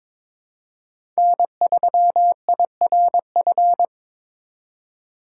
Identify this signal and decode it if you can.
Morse code, heard as a keyed tone, about 23 wpm, 700 Hz. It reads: N3IRF
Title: USA Call Signs - 5 Characters - 3x Character Spacing - 22wpm